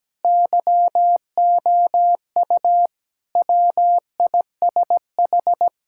Morse code, heard as a keyed tone, about 17 wpm, 700 Hz. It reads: YOU WISH
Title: Sentences from Top 1600 Words 17wpm